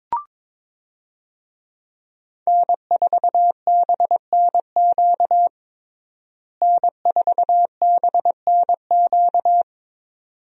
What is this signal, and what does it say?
Morse code, heard as a keyed tone, about 22 wpm, 700 Hz. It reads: N4BNQ N4BNQ
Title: USA Call Signs - 5 Characters - Repeated 2x - 22wpm